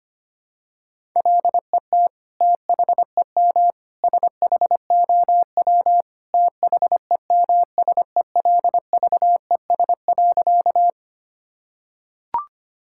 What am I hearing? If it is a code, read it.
Morse code, heard as a keyed tone, about 25 wpm, 700 Hz. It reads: LET THEM SHOW THEMSELVES.
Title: Sentences from Top 1000 Words - Review Part 5 - 25wpm